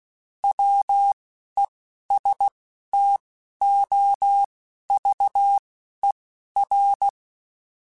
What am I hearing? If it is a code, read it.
Morse code, heard as a keyed tone, about 16 wpm, spaced as for 11 wpm, 800 Hz. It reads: WESTOVER